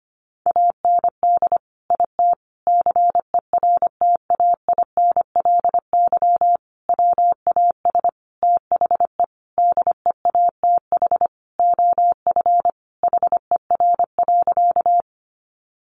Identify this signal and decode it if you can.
Morse code, heard as a keyed tone, about 25 wpm, 700 Hz. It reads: AND IT CERTAINLY WAS THE DEATH OF HER.
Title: Sentences from Top 1100 Words 25wpm